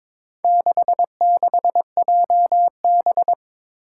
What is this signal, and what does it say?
Morse code, heard as a keyed tone, about 21 wpm, 700 Hz. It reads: 66JB
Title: Four Challenging Letter-Number Combinations 22wpm